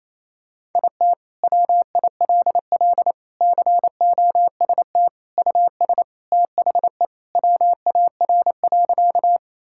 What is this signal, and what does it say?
Morse code, heard as a keyed tone, about 27 wpm, 700 Hz. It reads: IT WILL COST US THE WAR.
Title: Sentences from Top 1000 Words - Review Part 5 - 28wpm